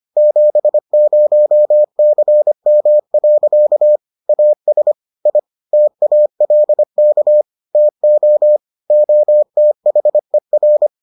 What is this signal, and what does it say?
Morse code, heard as a keyed tone, about 25 wpm, 600 Hz. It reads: 70CM. AS I TALK TO OTHER